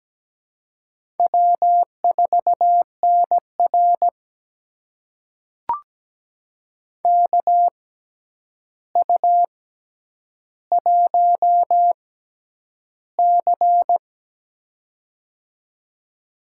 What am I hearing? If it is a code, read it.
Morse code, heard as a keyed tone, about 17 wpm, spaced as for 9 wpm, 700 Hz. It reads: W4NR KU1C
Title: USA Call Signs - 4 Characters - 6x Character Spacing - 17wpm